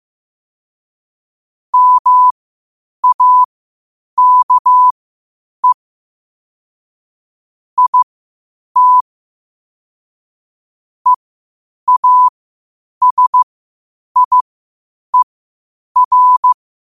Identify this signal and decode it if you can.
Morse code, heard as a keyed tone, about 15 wpm, spaced as for 8 wpm, 1000 Hz. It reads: MAKE IT EASIER
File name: qotdfiles / 8wpm